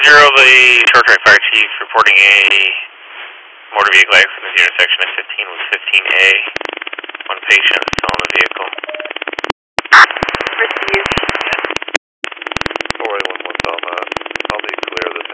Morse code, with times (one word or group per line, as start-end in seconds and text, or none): none